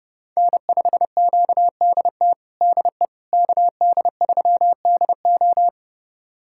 Morse code, heard as a keyed tone, about 30 wpm, 700 Hz. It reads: N5QDT DE KD3DO
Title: USA Callsign DE Callsign - 5 Characters - 30wpm